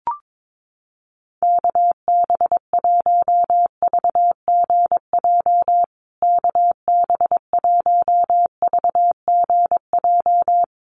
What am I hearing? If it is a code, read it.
Morse code, heard as a keyed tone, about 22 wpm, 700 Hz. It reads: KB1VGJ KB1VGJ
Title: USA Call Signs Repeated 2x 22wpm